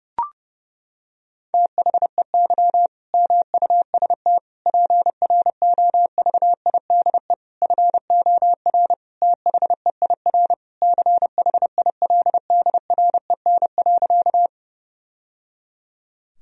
Morse code, heard as a keyed tone, about 32 wpm, 700 Hz. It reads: THEY MUST PROVIDE FOR THEIR CHILDREN.
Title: Sentences from Top 1000 Words - Review Part 3 - 30wpm